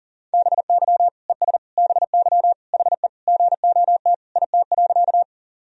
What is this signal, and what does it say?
Morse code, heard as a keyed tone, about 39 wpm, 700 Hz. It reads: BY ES BY HE GOT IT.
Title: Sentences from Top 300 Words and Abbreviations 40wpm